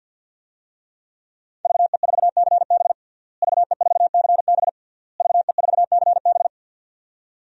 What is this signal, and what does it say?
Morse code, heard as a keyed tone, about 52 wpm, 700 Hz. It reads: VE4CB VE4CB VE4CB
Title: DXCC-100 Call Signs - 5 Characters - Repeated 3x - 50wpm